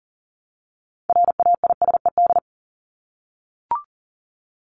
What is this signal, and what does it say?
Morse code, heard as a keyed tone, about 40 wpm, 700 Hz. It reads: RAISED
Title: ED Suffix - Top 100 Words with Base Words 40wpm